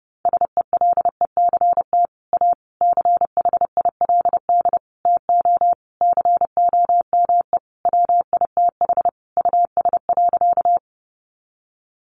Morse code, heard as a keyed tone, about 28 wpm, 700 Hz. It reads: SELECT A CHILD TO COME WITH US.
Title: Sentences from Top 1300 Words 30wpm